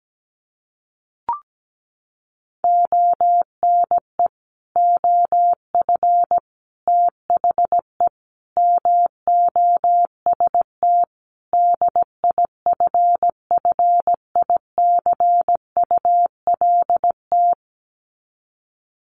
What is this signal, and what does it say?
Morse code, heard as a keyed tone, about 17 wpm, 700 Hz. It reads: ONE OF THE MOST DIFFICULT
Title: Sets of 5 Words Encore2 17wpm